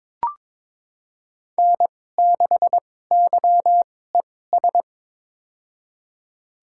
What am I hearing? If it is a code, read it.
Morse code, heard as a keyed tone, about 21 wpm, spaced as for 16 wpm, 700 Hz. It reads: N6YES
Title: USA Call Signs - 5 Characters - 2x Character Spacing - 22wpm